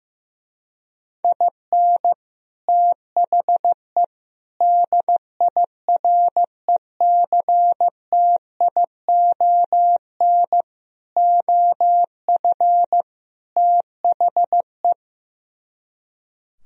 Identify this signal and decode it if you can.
Morse code, heard as a keyed tone, about 15 wpm, 700 Hz. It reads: IN THE DIRECTION OF THE
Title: Speed-Racing - Sets of 5 Words 15wpm